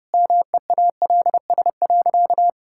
Morse code, heard as a keyed tone, about 30 wpm, 700 Hz. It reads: MEALS.
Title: Pangram Challenge v2 - 30wpm